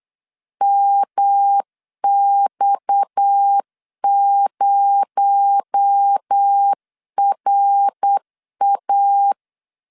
Morse code, heard as a keyed tone, about 8 wpm, 800 Hz. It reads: MX0RA
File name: LEZIONI / 6 PKX0 / 6 PKX0 8wpm 3341